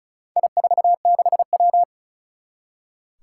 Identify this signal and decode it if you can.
Morse code, heard as a keyed tone, about 36 wpm, 700 Hz. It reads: I46W
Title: Four Challenging Letter-Number Combinations 35wpm